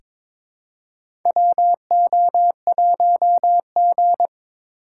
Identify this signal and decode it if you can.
Morse code, heard as a keyed tone, about 22 wpm, 700 Hz. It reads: WO1G
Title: USA Call Signs - 4 Characters - 2x Character Spacing - 22wpm